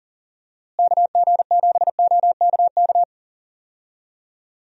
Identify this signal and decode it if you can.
Morse code, heard as a keyed tone, about 40 wpm, 700 Hz. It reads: KG7OKK
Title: USA Call Signs - 6 Characters - Repeated 2x - 40wpm